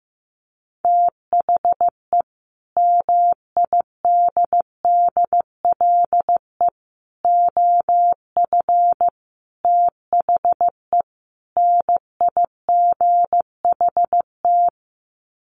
Morse code, heard as a keyed tone, about 15 wpm, 700 Hz. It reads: THE MIDDLE OF THE NIGHT
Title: Sets of 5 Words 15wpm